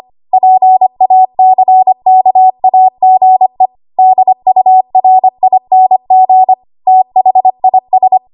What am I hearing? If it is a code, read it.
Morse code, heard as a keyed tone, about 25 wpm, 750 Hz. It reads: PACKAGE DURING THIS
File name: Archive / 25 WPM / 25WPM